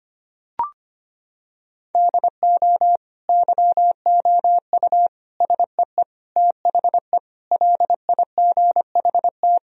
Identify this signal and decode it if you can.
Morse code, heard as a keyed tone, about 25 wpm, 700 Hz. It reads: DO YOU SEE THE LIGHT
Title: Sentences from Top 1000 Words - Review Part 1 - 25wpm